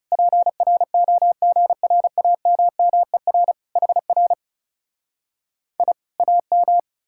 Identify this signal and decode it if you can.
Morse code, heard as a keyed tone, about 40 wpm, 700 Hz. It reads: PROGRAMMER HR I AM